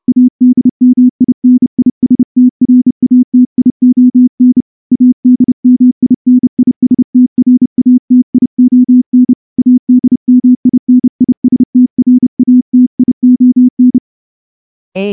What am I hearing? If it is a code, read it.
Morse code, heard as a keyed tone, about 30 wpm, 250 Hz. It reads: ADMINISTRATION ADMINISTRATION ADMINISTRATION
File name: morse-training-30WPM